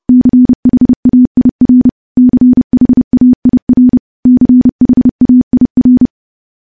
Morse code, heard as a keyed tone, about 30 wpm, 250 Hz. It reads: CHAIR CHAIR CHAIR
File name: morse / morse-training-30WPM